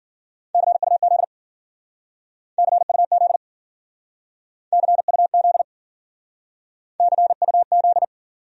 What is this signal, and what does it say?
Morse code, heard as a keyed tone, about 48 wpm, 700 Hz. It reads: CUZ CUZ CUZ CUZ
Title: Speed-Racing Abbreviations - 40wpm